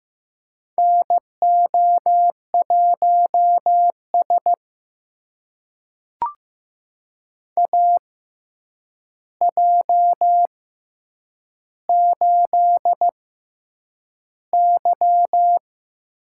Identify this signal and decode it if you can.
Morse code, heard as a keyed tone, about 15 wpm, spaced as for 8 wpm, 700 Hz. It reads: NO1S AJ8Y
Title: USA Call Signs - 4 Characters - 6x Character Spacing - 15wpm